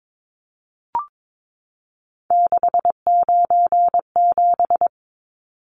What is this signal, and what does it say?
Morse code, heard as a keyed tone, about 22 wpm, 700 Hz. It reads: 697
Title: Numbers - 3-Digit 22wpm